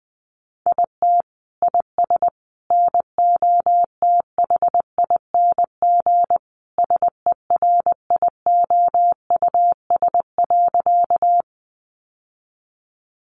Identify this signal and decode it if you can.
Morse code, heard as a keyed tone, about 20 wpm, 700 Hz. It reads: IT IS NOTHING SERIOUS.
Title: Sentences from Top 1100 Words 20wpm